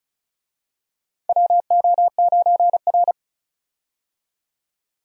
Morse code, heard as a keyed tone, about 34 wpm, 700 Hz. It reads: WO9R
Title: USA Call Signs - 4 Characters - 35wpm